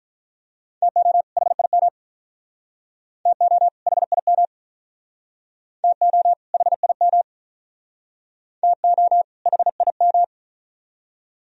Morse code, heard as a keyed tone, about 46 wpm, 700 Hz. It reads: TO HIM TO HIM TO HIM TO HIM